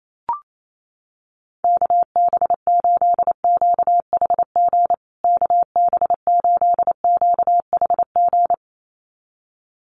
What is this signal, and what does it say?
Morse code, heard as a keyed tone, about 28 wpm, 700 Hz. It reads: KB8QHG KB8QHG
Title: USA Call Signs Repeated 2x 28wpm